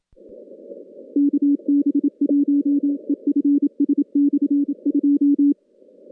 Morse code, heard as a keyed tone, about 27 wpm, 280 Hz. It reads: KB1EFS/2
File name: contests / CW-SS-k5tr- -2200